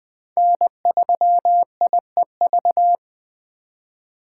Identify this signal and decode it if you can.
Morse code, heard as a keyed tone, about 20 wpm, 700 Hz. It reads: N3IEV
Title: USA Call Signs - 8x Character Spacing - 20wpm